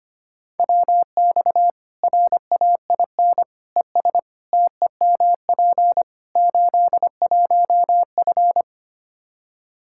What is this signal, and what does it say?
Morse code, heard as a keyed tone, about 25 wpm, 700 Hz. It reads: WX RAIN ES TEMP 81F
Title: Practice QSO - Exchange 2 - Info - 25wpm